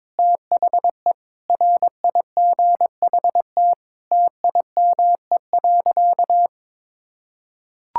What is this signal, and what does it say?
Morse code, heard as a keyed tone, about 22 wpm, 700 Hz. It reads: THE RIGHT TIME.
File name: Sentences with 9-Word Phrases - 22wpm